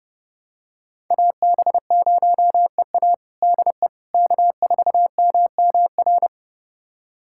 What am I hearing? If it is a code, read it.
Morse code, heard as a keyed tone, about 30 wpm, 700 Hz. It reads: AB0EA DE K4MMR
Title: USA Callsign DE Callsign - 5 Characters - 30wpm